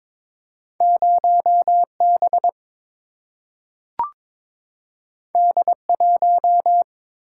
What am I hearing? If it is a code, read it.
Morse code, heard as a keyed tone, about 22 wpm, 700 Hz. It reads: 0 B D 1
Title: Single-Letter Single-Number 22wpm